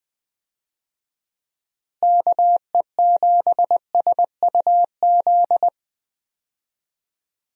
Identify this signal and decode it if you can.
Morse code, heard as a keyed tone, about 21 wpm, 700 Hz. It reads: KE7SUZ
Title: USA Call Signs - 6 Characters - Repeated 2x - 20wpm